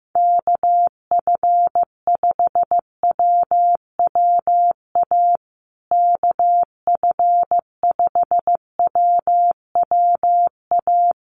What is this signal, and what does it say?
Morse code, heard as a keyed tone, about 15 wpm, 700 Hz. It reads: KF5WWA KF5WWA
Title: USA Call Signs Repeated 3x 15wpm